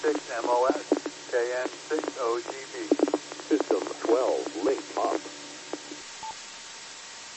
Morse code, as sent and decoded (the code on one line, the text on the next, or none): none